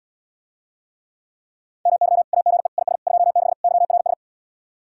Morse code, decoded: MY AGE IS 26 YRS

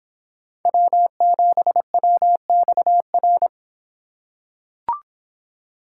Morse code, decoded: W7WXR

